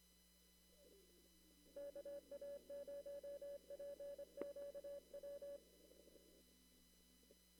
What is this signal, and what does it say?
Morse code, heard as a keyed tone, about 21 wpm, 570 Hz. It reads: KA0PQW
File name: CW / kt5j wpx cw-op-k5pi